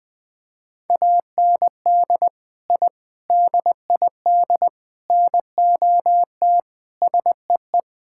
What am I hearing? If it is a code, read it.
Morse code, heard as a keyed tone, about 20 wpm, 700 Hz. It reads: AND I DID NOT SEE
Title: Sentences with 4-Word Phrases - 20wpm